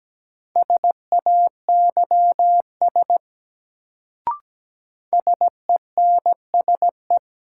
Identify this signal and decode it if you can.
Morse code, heard as a keyed tone, about 17 wpm, 700 Hz. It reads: SAYS SENSE
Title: Sets of 2 Words - New Word Drill - 17wpm